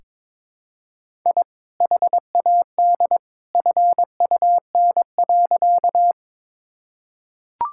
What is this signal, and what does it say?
Morse code, heard as a keyed tone, about 21 wpm, 700 Hz. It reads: I HAD FUN.
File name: practice / 22wpm / Sentences with 2-Word Phrases - 22wpm